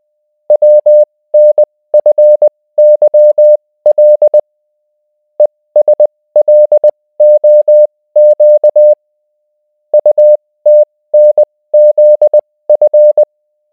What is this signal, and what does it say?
Morse code, heard as a keyed tone, about 20 wpm, spaced as for 16 wpm, 600 Hz. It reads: WNFYL ESLOQ UTNZF